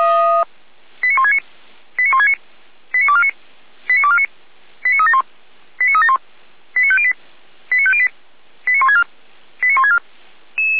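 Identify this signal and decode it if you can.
Five-tone selective call (ZVEI), heard as quick runs of five tones, tones of about 70 ms. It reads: 89169 89169 89249 89249 89371 89371 89508 89508 89164 89164
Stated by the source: ZVEI five-tone call 89167 received by SDR for the Orlamünde fire brigade (feuerwehr-orlamuende).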